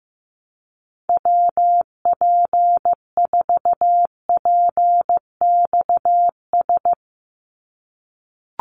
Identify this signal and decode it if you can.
Morse code, heard as a keyed tone, about 16 wpm, 700 Hz. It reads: WP4PXS
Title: USA Call Signs 15wpm